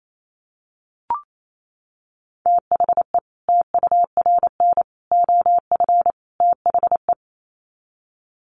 Morse code, heard as a keyed tone, about 28 wpm, 700 Hz. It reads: THE TURN OF THE
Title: Sets of 4 Words 28wpm